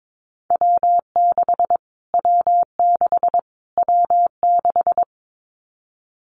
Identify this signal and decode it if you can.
Morse code, heard as a keyed tone, about 22 wpm, 700 Hz. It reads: W6 W6 W6
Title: Callsign Prefixes for Top 25 DXCC - Repeated 3x - 22wpm